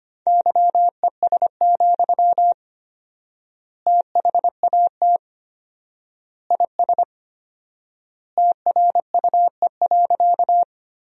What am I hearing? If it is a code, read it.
Morse code, heard as a keyed tone, about 25 wpm, 700 Hz. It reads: YES, THAT IS TRUE.
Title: 4x Word-Spacing - Sentences from Top 500 Words - 25wpm